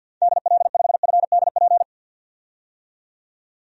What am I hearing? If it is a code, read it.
Morse code, heard as a keyed tone, about 53 wpm, 700 Hz. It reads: DL5FDP
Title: DXCC-100 Call Signs - 6 Characters - 50wpm